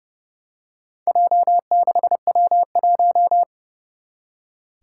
Morse code, heard as a keyed tone, about 30 wpm, 700 Hz. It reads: J6W1
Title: Four Challenging Letter-Number Combinations 30wpm